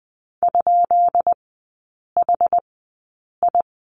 Morse code, heard as a keyed tone, about 19 wpm, spaced as for 8 wpm, 700 Hz. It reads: ?HI
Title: 133 Lesson 22e - Question Mark - Three Characters 20wpm